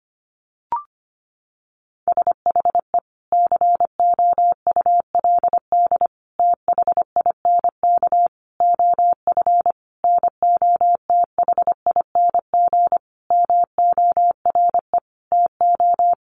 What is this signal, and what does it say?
Morse code, heard as a keyed tone, about 25 wpm, 700 Hz. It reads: SHE COULD THINK OF NOTHING MORE TO
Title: Sentences from Top 500 Words 25wpm